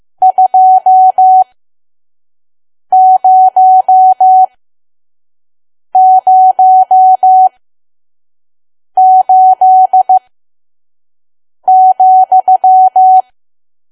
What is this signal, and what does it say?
Morse code, heard as a keyed tone, about 15 wpm, spaced as for 5 wpm, 750 Hz. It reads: 2008,
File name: Archive / 5 WPM files / PM